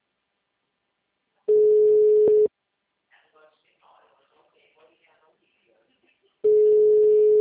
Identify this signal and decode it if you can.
Ringback tone (Europe, 425 Hz), with speech.